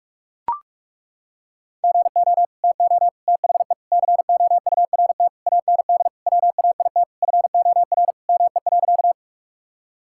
Morse code, heard as a keyed tone, about 42 wpm, 700 Hz. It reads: GO TO THE COURT AND WAIT FOR ME.
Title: Sentences from Top 1600 Words 45wpm